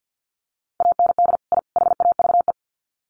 Morse code, heard as a keyed tone, about 50 wpm, 700 Hz. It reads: AND I HAVE
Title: Sets of 3 Words 50wpm